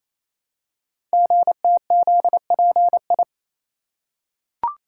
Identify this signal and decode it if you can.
Morse code, heard as a keyed tone, about 29 wpm, 700 Hz. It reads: GTZPI
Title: Code Groups - 5 Letters - 28wpm